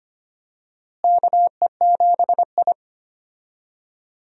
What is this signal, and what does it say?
Morse code, heard as a keyed tone, about 25 wpm, 700 Hz. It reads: KE7I
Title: USA Call Signs - 4 Characters - Repeated 2x - 25wpm